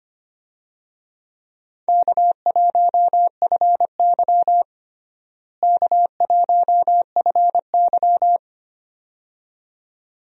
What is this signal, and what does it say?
Morse code, heard as a keyed tone, about 25 wpm, 700 Hz. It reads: K1FY K1FY
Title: USA Call Signs - 4 Characters - Repeated 2x - 25wpm